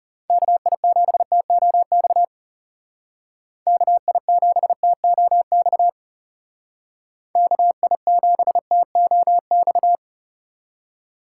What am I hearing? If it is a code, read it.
Morse code, heard as a keyed tone, about 36 wpm, 700 Hz. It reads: KI7TOX KI7TOX KI7TOX
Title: Speed Racing - USA 2x3 Call Signs 30wpm